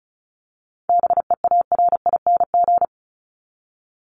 Morse code, heard as a keyed tone, about 34 wpm, 700 Hz. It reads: BEARING
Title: ING Suffix - Top 100 Words with Base Words 35wpm